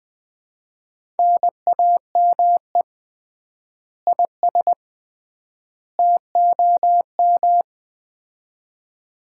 Morse code, heard as a keyed tone, about 20 wpm, 700 Hz. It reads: NAME IS TOM
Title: Common QSO Phrases - 3x Word-Spacing - 20wpm